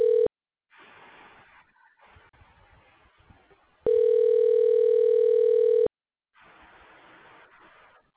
Ringback tone (North America), 440 + 480 Hz.
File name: recordings / RE3a08ab a4f9ac b